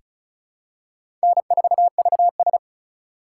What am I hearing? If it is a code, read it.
Morse code, heard as a keyed tone, about 36 wpm, 700 Hz. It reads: N4VS
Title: USA Call Signs - 4 Characters - 4x Character Spacing - 35wpm